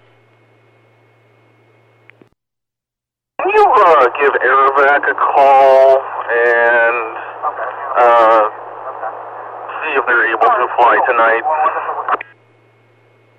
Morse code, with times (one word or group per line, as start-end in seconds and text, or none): none